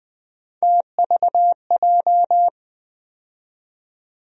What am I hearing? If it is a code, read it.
Morse code, heard as a keyed tone, about 20 wpm, 700 Hz. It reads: TVJ
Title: USA Call Sign Postfixes - 3 Letters 20wpm